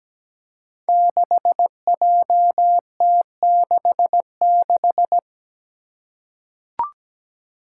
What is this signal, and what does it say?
Morse code, heard as a keyed tone, about 17 wpm, 700 Hz. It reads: 6JT66
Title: Five Challenging Letter-Number Combinations 17wpm